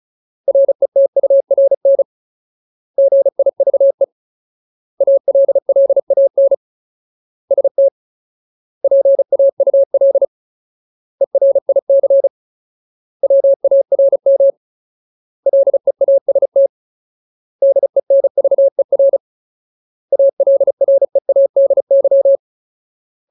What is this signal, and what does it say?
Morse code, heard as a keyed tone, about 34 wpm, 550 Hz. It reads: RETURN GIVE ALLAN ST PAUL ERIC WARM LEAST DENVER ALREADY
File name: Ultimate Word List / 35WPM Ultimate